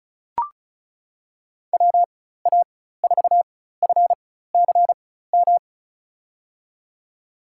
Morse code, heard as a keyed tone, about 34 wpm, spaced as for 16 wpm, 700 Hz. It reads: WA4FCM
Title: USA Call Signs - 6 Characters - 4x Character Spacing - 35wpm